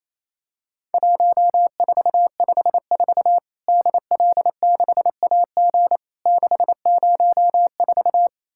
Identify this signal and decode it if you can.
Morse code, heard as a keyed tone, about 28 wpm, 700 Hz. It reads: 1454 DL6AG 604
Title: Worked All Europe DX Contest - QTC Exchange - 28wpm